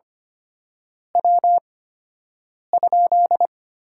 Morse code, heard as a keyed tone, about 25 wpm, spaced as for 7 wpm, 700 Hz. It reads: W?
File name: Top 90 QSO Elements - 8x Character Spacing - 25wpm